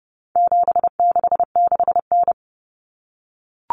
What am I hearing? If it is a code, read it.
Morse code, heard as a keyed tone, about 30 wpm, 700 Hz. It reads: 766N
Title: Common Cut Numbers - 4-Digit 30wpm